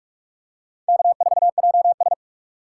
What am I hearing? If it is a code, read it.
Morse code, heard as a keyed tone, about 46 wpm, 700 Hz. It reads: K4JS
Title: USA Call Signs - 3x Character Spacing - 45wpm